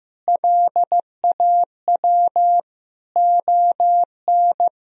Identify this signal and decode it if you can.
Morse code, heard as a keyed tone, about 15 wpm, 700 Hz. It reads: LAW ON
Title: Sentences from Top 1000 Words - Review Part 3 - 15wpm